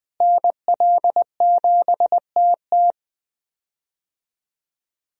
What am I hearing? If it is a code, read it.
Morse code, heard as a keyed tone, about 20 wpm, 700 Hz. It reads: NL7TT
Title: USA Call Signs - 5 Characters - Repeated 2x - 20wpm